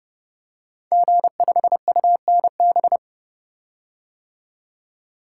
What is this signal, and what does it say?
Morse code, heard as a keyed tone, about 30 wpm, 700 Hz. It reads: G5UNB
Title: Code Groups - 5 Letter:Numbers - 30wpm